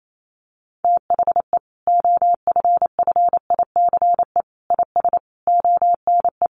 Morse code, heard as a keyed tone, about 28 wpm, 700 Hz. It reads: THE OFFICE IS ONE
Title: Sentences from Top 1000 Words 28wpm